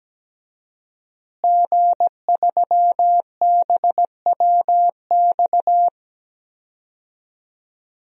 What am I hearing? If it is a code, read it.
Morse code, heard as a keyed tone, about 17 wpm, 700 Hz. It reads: G3BWX
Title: DXCC-100 Call Signs - 5 Characters - Repeated 3x - 17wpm